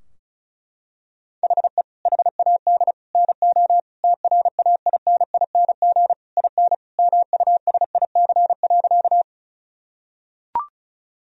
Morse code, heard as a keyed tone, about 34 wpm, 700 Hz. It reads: HE HAD NO TRAINING IN MUSIC.
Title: Sentences from Top 1000 Words - Review Part 4 - 35wpm